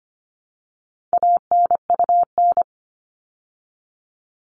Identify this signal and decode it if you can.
Morse code, heard as a keyed tone, about 25 wpm, 700 Hz. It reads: ANUN
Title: All Cut Numbers - 4-Digit 25wpm